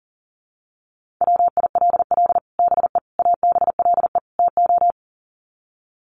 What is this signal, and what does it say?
Morse code, heard as a keyed tone, about 40 wpm, 700 Hz. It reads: WILL BE ABLE TO
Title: Sets of 4 Words 40wpm